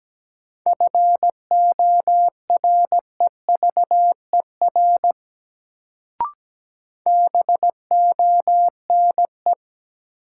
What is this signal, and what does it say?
Morse code, heard as a keyed tone, about 17 wpm, 700 Hz. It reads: FOREVER BONE
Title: Sentences from Top 1600 Words - New Word Drill - 17wpm